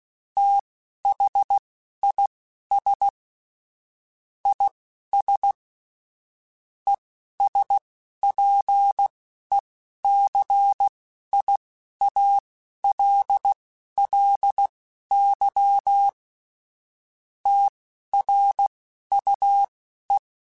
Morse code, heard as a keyed tone, about 16 wpm, spaced as for 11 wpm, 800 Hz. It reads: THIS IS ESPECIALLY TRUE
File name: Technical Subjects / Technical Subjects-mp3s / AND Gate @10